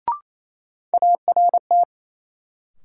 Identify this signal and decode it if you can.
Morse code, heard as a keyed tone, about 28 wpm, 700 Hz. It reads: ART
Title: Sentences from Top 600 Words - New Word Drill - 28wpm